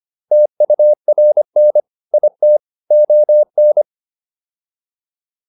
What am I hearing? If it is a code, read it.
Morse code, heard as a keyed tone, about 25 wpm, 600 Hz. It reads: TURN IT ON